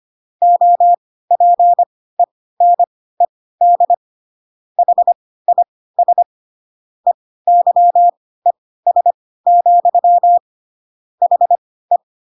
Morse code, heard as a keyed tone, about 25 wpm, spaced as for 16 wpm, 700 Hz. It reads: OPENED HIS EYES, HE
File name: Pinocchio / Pinocchio ch1 10 25 ik1hge